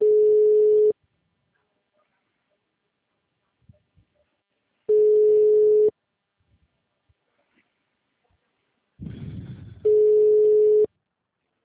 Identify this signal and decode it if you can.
Ringback tone (Europe, 425 Hz).
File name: out- -2013- - - .8388